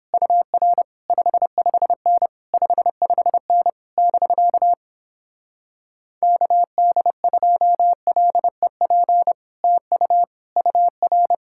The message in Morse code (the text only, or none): UR 55N 55N <BK> KD2LEP TU UR